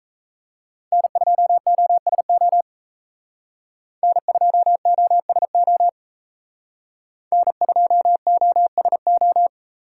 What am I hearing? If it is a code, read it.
Morse code, heard as a keyed tone, about 37 wpm, 700 Hz. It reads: N2OSO N2OSO N2OSO